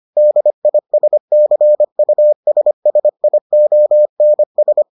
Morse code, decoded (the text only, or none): DISCUSSIONS